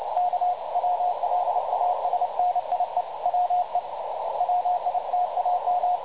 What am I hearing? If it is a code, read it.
Morse code, heard as a keyed tone, about 29 wpm, 720 Hz, with radio noise. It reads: K WLASNIE WE WITAM
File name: dlazaawansowanych / nagraniazpasma / websdr recording 3565.0kHz